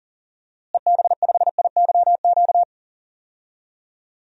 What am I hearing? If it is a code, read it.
Morse code, heard as a keyed tone, about 37 wpm, 700 Hz. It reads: EB5IYQ
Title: DXCC-100 Call Signs - 6 Characters - Repeated 2x - 40wpm